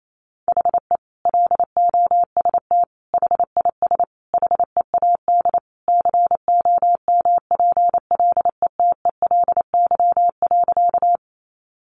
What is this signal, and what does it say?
Morse code, heard as a keyed tone, about 28 wpm, 700 Hz. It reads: HE LOST HIS HEAD COMPLETELY.
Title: Sentences from Top 1400 Words 28wpm